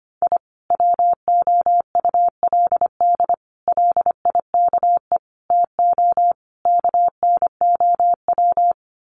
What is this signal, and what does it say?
Morse code, heard as a keyed tone, about 25 wpm, 700 Hz. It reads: I WOULD LIKE TO KNOW
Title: Sets of 5 Words Encore 25wpm